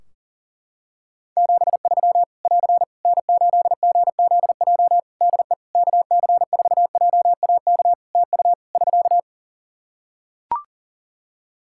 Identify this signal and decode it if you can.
Morse code, heard as a keyed tone, about 40 wpm, 700 Hz. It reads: 73 <AR> N8GZJ DE KC4JAK TU <SK>
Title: Practice QSO - Exchange 4 - Ending - 40wpm